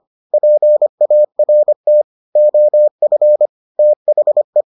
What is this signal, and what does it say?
Morse code, heard as a keyed tone, about 25 wpm, 600 Hz. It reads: PART OF THE